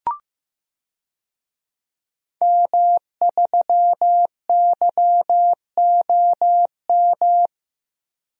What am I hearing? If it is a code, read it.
Morse code, heard as a keyed tone, about 15 wpm, 700 Hz. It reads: M3YOM
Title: DXCC-100 Call Signs - 5 Characters - Repeated 3x - 15wpm